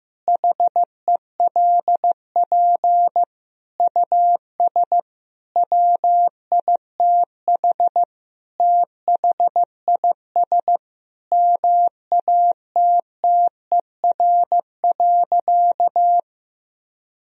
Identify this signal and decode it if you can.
Morse code, heard as a keyed tone, about 15 wpm, 700 Hz. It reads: HELP US WITH THIS MATTER.
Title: Sentences from Top 1000 Words - Review Part 2 - 15wpm